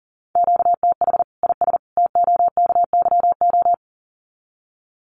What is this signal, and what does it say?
Morse code, heard as a keyed tone, about 40 wpm, 700 Hz. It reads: QTH IS TOKYO
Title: Common QSO Phrases 40wpm